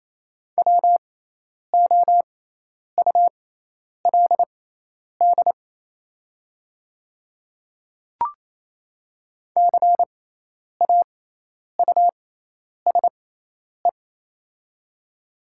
Morse code, heard as a keyed tone, about 28 wpm, spaced as for 10 wpm, 700 Hz. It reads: WOULD CAUSE
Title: 6x Character Spacing - Top 200 Words - 28wpm